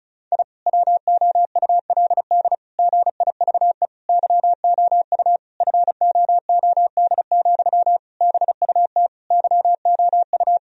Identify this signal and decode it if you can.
Morse code, heard as a keyed tone, about 34 wpm, 700 Hz. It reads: I WOULD GIVE YOU FOOD, BUT YOU
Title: Sentences from Top 1000 Words - Review Part 1 - 35wpm